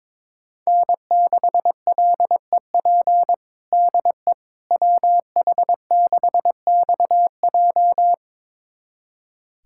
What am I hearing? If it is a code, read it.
Morse code, heard as a keyed tone, about 22 wpm, 700 Hz. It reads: N6LEP DE WH6XJ
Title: USA Callsign DE Callsign - 5 Characters - 22wpm